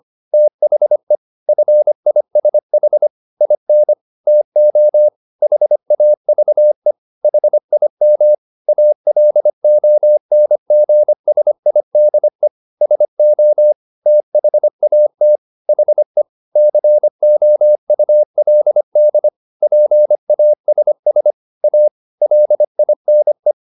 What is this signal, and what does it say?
Morse code, heard as a keyed tone, about 25 wpm, 600 Hz. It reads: THE FISH IN TO HAVE HIM ALONGSIDE SO THAT HE COULD PASS A LINE